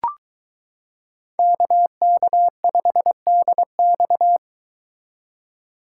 Morse code, heard as a keyed tone, about 22 wpm, 700 Hz. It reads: KK5DX